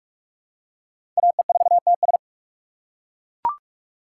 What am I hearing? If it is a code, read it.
Morse code, heard as a keyed tone, about 42 wpm, 700 Hz. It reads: AE4TS
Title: USA Call Signs - 3x Character Spacing - 45wpm